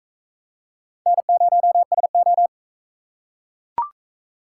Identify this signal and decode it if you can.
Morse code, heard as a keyed tone, about 42 wpm, 700 Hz. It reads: N0SO